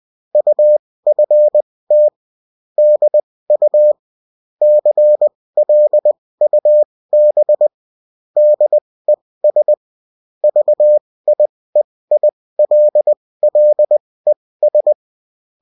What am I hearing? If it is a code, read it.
Morse code, heard as a keyed tone, about 20 wpm, spaced as for 16 wpm, 600 Hz. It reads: UFT DU CLUB DES VIEILLES